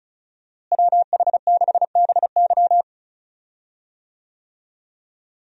Morse code, heard as a keyed tone, about 36 wpm, 700 Hz. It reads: WH6BY
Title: USA Call Signs - 5 Characters - Repeated 2x - 35wpm